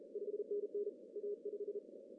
Morse code, heard as a keyed tone, about 36 wpm, 410 Hz. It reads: 5NN A5
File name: CW / cqww-cw-k5tr